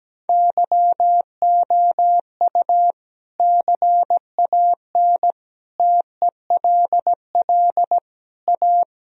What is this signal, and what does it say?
Morse code, heard as a keyed tone, about 17 wpm, 700 Hz. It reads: YOU CAN TELL A
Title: Sentences from Top 1000 Words - Review Part 3 - 17wpm